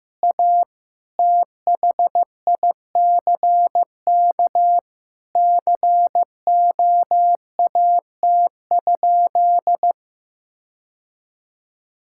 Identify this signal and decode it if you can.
Morse code, heard as a keyed tone, about 15 wpm, 700 Hz. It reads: A THICK COAT?
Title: Sentences from Top 1600 Words 15wpm